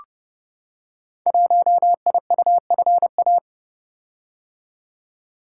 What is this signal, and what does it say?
Morse code, heard as a keyed tone, about 30 wpm, 700 Hz. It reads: 1IUFA
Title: Code Groups - 5 Letter:Numbers - 30wpm